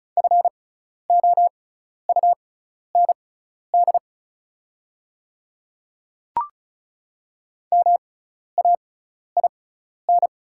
Morse code, heard as a keyed tone, about 36 wpm, spaced as for 12 wpm, 690 Hz. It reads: FOUND MAIN
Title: 6x Character Spacing - Top 300 Words - 35wpm